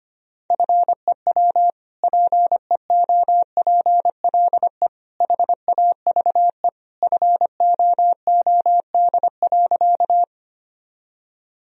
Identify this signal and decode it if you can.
Morse code, heard as a keyed tone, about 25 wpm, 700 Hz. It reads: FEW PEOPLE HAVE FOOD.
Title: Sentences from Top 1000 Words - Review Part 1 - 25wpm